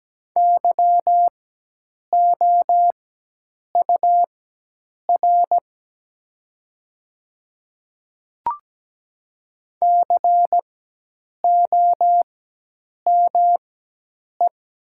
Morse code, heard as a keyed tone, about 17 wpm, spaced as for 8 wpm, 700 Hz. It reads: YOUR COME
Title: 4x Character Spacing - Top 200 Words - 17wpm